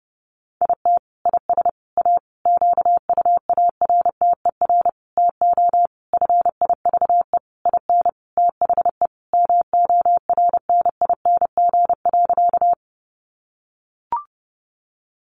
Morse code, heard as a keyed tone, about 30 wpm, 700 Hz. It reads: IT IS A QUARTER TO FIVE IN THE MORNING.
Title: Sentences from Top 1300 Words 30wpm